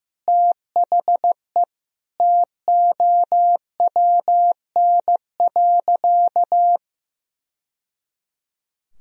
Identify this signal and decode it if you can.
Morse code, heard as a keyed tone, about 16 wpm, 700 Hz. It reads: THE TOWN.
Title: Sentences from Top 1000 Words - Review Part 4 - 15wpm